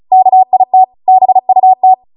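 Morse code, heard as a keyed tone, about 36 wpm, 750 Hz. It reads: KIT BUT
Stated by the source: Amateur radio Morse code practice archive transmission at 35 wpm.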